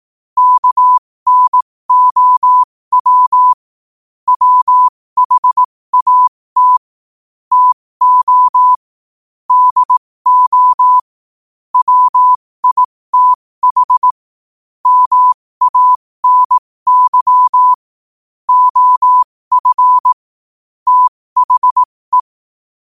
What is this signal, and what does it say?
Morse code, heard as a keyed tone, about 18 wpm, spaced as for 15 wpm, 1000 Hz. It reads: KNOW WHAT TO DO WITH MANY OF THE
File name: qotdfiles / 15wpm